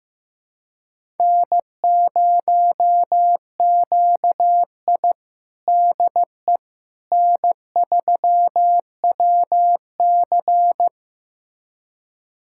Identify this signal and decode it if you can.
Morse code, heard as a keyed tone, about 16 wpm, 700 Hz. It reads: N0QI DE N3WC
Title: USA Callsign DE Callsign - 4 Characters - 15wpm